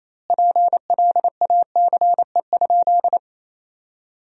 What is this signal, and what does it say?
Morse code, heard as a keyed tone, about 28 wpm, 700 Hz. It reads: PLACE?
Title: 2x Word-Spacing - Sentences from Top 200 Words - 28wpm